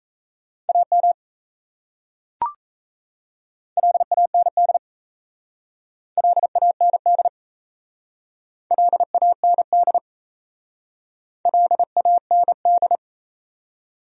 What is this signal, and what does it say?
Morse code, heard as a keyed tone, about 34 wpm, 700 Hz. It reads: AM LAND LAND LAND LAND
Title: Speed-Racing Top 500 Words 28wpm